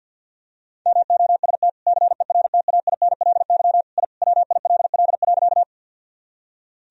Morse code, heard as a keyed tone, about 50 wpm, 700 Hz. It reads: MOST CERTAINLY I WILL.
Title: Sentences from Top 1000 Words - Review Part 1 - 50wpm